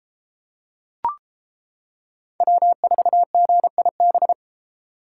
Morse code, heard as a keyed tone, about 31 wpm, 700 Hz. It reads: W4GIB